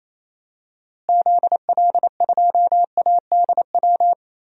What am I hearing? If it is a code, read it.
Morse code, heard as a keyed tone, about 28 wpm, 700 Hz. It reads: ZL2ADW
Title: DXCC-100 Call Signs - 6 Characters - Repeated 3x - 28wpm